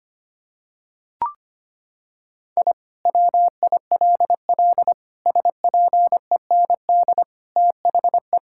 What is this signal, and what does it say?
Morse code, heard as a keyed tone, about 25 wpm, 700 Hz. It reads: I WILL SPEND THE
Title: Sentences from Top 1000 Words - Review Part 5 - 25wpm